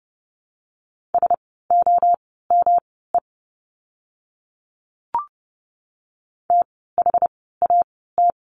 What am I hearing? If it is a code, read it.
Morse code, heard as a keyed tone, about 30 wpm, spaced as for 17 wpm, 700 Hz. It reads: SOME THAT
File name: course / 30:10wpm / 237 Lesson 38f - Letter q - Words 30wpm